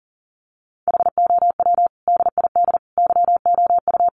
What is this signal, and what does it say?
Morse code, heard as a keyed tone, about 40 wpm, 700 Hz. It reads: HOW DID YOU
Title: Sentences from Top 1500 Words 40wpm